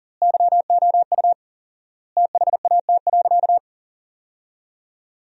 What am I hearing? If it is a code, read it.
Morse code, heard as a keyed tone, about 40 wpm, 700 Hz. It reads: YOU THAT.
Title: 4x Word-Spacing - Sentences from Top 300 Words - 40wpm